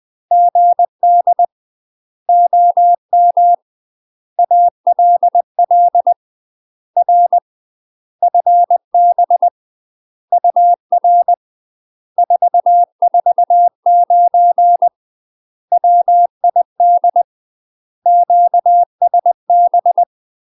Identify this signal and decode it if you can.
Morse code, heard as a keyed tone, about 20 wpm, 700 Hz. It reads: GD OM ALL R FB UR 449 WID QSB
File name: UK MASTER QSOs TRAINING / 20WPM / UK MASTER QSOs 0072